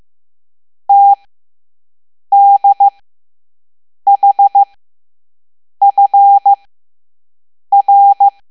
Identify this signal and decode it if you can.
Morse code, heard as a keyed tone, about 15 wpm, spaced as for 6 wpm, 800 Hz. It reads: TDHFR
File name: Morse / Level7-All-A